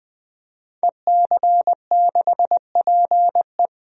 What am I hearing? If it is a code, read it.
Morse code, heard as a keyed tone, about 20 wpm, 700 Hz. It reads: EC6PE
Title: DXCC-100 Call Signs - 5 Characters - 20wpm